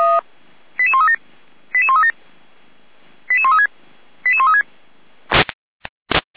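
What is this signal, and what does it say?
Five-tone selective call (ZVEI), heard as quick runs of five tones, tones of about 70 ms. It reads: 88137 88137 88136 88136
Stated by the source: ZVEI five-tone selective call received on an SDR.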